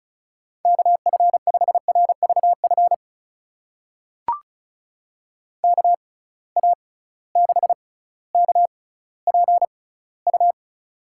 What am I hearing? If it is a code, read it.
Morse code, heard as a keyed tone, about 36 wpm, spaced as for 18 wpm, 700 Hz. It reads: KF5RVF KA6KPU
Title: USA Call Signs - 6 Characters - 6x Character Spacing - 35wpm